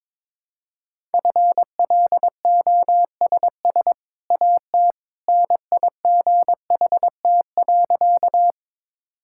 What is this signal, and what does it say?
Morse code, heard as a keyed tone, about 21 wpm, 700 Hz. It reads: FLOSS AT NIGHT.
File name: practice / 22wpm / Easy Sentences with Hints - 22wpm